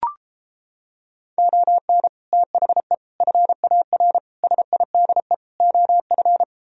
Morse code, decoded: ON THE FAR SIDE OF

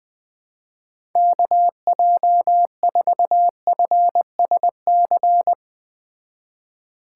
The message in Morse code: KJ4FSC

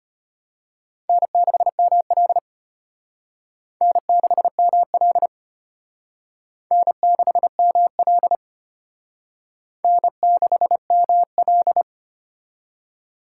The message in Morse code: N6ML N6ML N6ML N6ML

